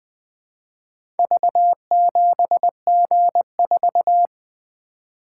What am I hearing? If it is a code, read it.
Morse code, heard as a keyed tone, about 20 wpm, 700 Hz. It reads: V7G4
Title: Four Letter-Number Combinations - 20wpm